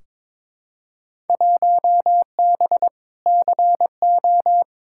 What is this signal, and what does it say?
Morse code, heard as a keyed tone, about 21 wpm, 700 Hz. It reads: 1B CO
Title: Field Day Exchanges - 22wpm